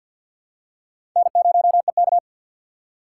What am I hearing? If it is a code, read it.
Morse code, heard as a keyed tone, about 53 wpm, 700 Hz. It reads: N0EK
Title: USA Call Signs - 6x Character Spacing - 50wpm